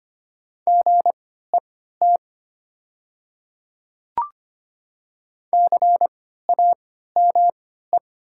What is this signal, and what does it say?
Morse code, heard as a keyed tone, about 25 wpm, spaced as for 14 wpm, 700 Hz. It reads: GET CAME
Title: 3x Character Spacing - Top 200 Words - 25wpm